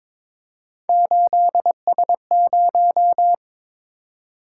Morse code, heard as a keyed tone, about 21 wpm, 700 Hz. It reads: 8S0
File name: practice / 22wpm / Three Letter-Number Combinations - 22wpm